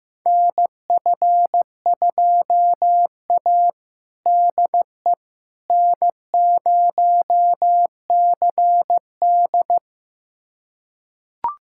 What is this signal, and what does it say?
Morse code, heard as a keyed tone, about 15 wpm, 700 Hz. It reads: NF2A DE N0CD
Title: USA Callsign DE Callsign - 4 Characters - 15wpm